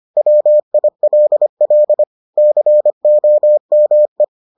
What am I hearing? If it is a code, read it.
Morse code, heard as a keyed tone, about 25 wpm, 600 Hz. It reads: WILL COME